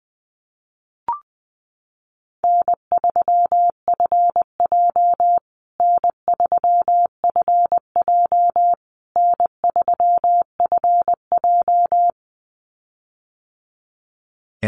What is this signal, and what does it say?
Morse code, heard as a keyed tone, about 19 wpm, 700 Hz. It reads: N3FJ N3FJ N3FJ
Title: USA Call Signs Repeated 3x 20wpm